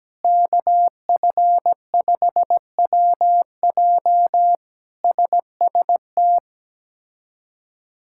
Morse code, heard as a keyed tone, about 17 wpm, 700 Hz. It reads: KF5WJ SST
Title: K1USN SST QSO Practice - 17wpm